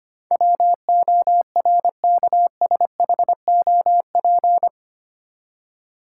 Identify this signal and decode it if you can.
Morse code, heard as a keyed tone, about 25 wpm, 700 Hz. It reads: WORKSHOP
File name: Eight Letter Words - 25wpm